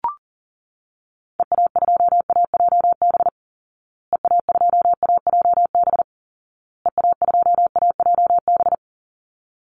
Morse code, heard as a keyed tone, about 40 wpm, 700 Hz. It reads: EA2AJB EA2AJB EA2AJB
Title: Call Signs Repeated 2x - 40wpm